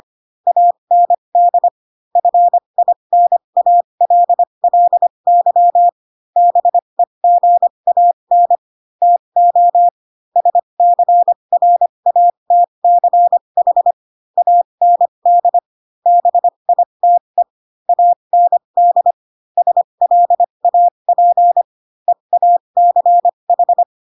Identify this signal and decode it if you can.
Morse code, heard as a keyed tone, about 25 wpm, spaced as for 22 wpm, 700 Hz. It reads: AND FINALLY BEGAN TO SCRATCH AND BITE AND SLAP EACH
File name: Pinocchio / Pinocchio ch2 18 25 ik1hge